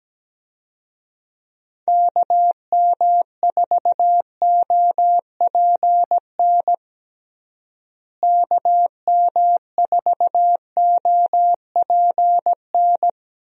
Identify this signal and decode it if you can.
Morse code, heard as a keyed tone, about 17 wpm, 700 Hz. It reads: KM4OPN KM4OPN
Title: USA Call Signs - 6 Characters - Repeated 2x - 17wpm